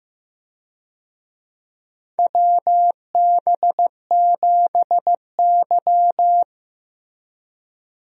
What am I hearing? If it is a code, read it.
Morse code, heard as a keyed tone, about 16 wpm, 700 Hz. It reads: WB7Y
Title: USA Call Signs - 4 Characters - Repeated 2x - 15wpm